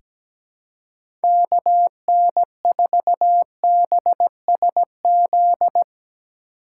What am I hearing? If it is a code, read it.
Morse code, heard as a keyed tone, about 17 wpm, 700 Hz. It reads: KN4BSZ
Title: USA Call Signs - 6x Character Spacing - 17wpm